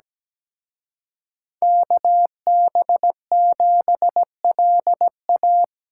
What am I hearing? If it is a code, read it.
Morse code, heard as a keyed tone, about 20 wpm, 700 Hz. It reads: KB7LA
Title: Speed Racing - USA 2x2 Call Signs 17wpm